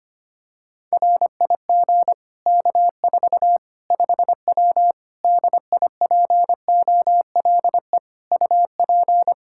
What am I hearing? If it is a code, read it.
Morse code, heard as a keyed tone, about 25 wpm, 700 Hz. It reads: RIG K4 5W DIPOLE UP
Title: Practice QSO - Exchange 2 - Info - 25wpm